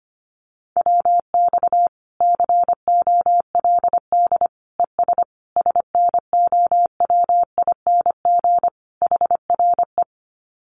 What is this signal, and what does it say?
Morse code, heard as a keyed tone, about 25 wpm, 700 Hz. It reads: WX COLD ES SNOWING HRE
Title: Common QSO Phrases 25wpm